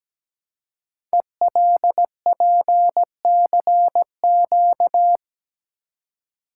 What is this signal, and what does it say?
Morse code, heard as a keyed tone, about 17 wpm, 700 Hz. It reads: ELPCQ
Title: Code Groups - 5 Letters - 17wpm